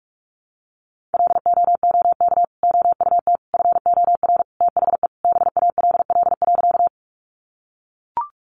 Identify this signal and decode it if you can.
Morse code, heard as a keyed tone, about 44 wpm, 700 Hz. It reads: LOOK OUT FOR THE BALL.
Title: Sentences from Top 900 Words 45wpm